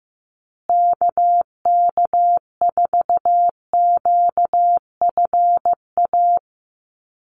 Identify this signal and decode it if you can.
Morse code, heard as a keyed tone, about 15 wpm, 700 Hz. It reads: KK4QFA
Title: USA Call Signs Repeated 3x 15wpm